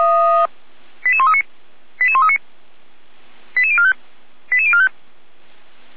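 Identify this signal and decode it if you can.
Five-tone selective call (ZVEI), heard as quick runs of five tones, tones of about 70 ms. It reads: 88139 88139 88045 88045
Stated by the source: SDR-received ZVEI five-tone call.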